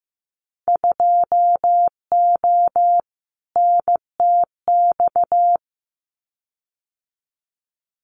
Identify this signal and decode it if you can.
Morse code, heard as a keyed tone, about 15 wpm, 700 Hz. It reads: 2O NTX
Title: Winter Field Day Exchanges Encore 15wpm